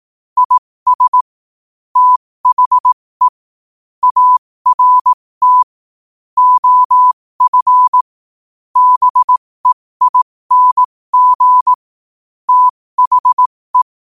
Morse code, heard as a keyed tone, about 18 wpm, spaced as for 15 wpm, 1000 Hz. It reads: IS THE ART OF BEING THE
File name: qotdfiles / 15wpm